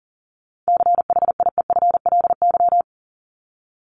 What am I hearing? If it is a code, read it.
Morse code, heard as a keyed tone, about 40 wpm, 700 Hz. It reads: CHIEFLY
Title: LY Suffix - Top 100 Words with Base Words 40wpm